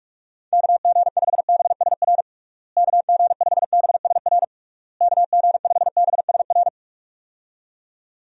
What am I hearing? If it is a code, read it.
Morse code, heard as a keyed tone, about 46 wpm, 700 Hz. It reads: KG5BSR KG5BSR KG5BSR
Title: USA Call Signs - 6 Characters - Repeated 3x - 45wpm